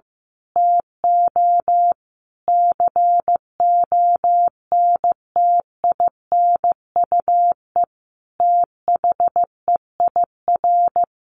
Morse code, heard as a keyed tone, about 15 wpm, 700 Hz. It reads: TO CONTINUE THEIR
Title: Sentences from Top 1100 Words 15wpm